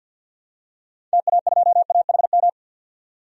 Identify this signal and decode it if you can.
Morse code, heard as a keyed tone, about 51 wpm, 700 Hz. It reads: TA2AHM
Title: DXCC-100 Call Signs - 6 Characters - 50wpm